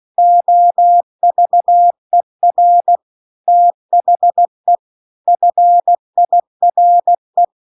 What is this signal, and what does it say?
Morse code, heard as a keyed tone, about 16 wpm, 700 Hz. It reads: OVER THE FIRE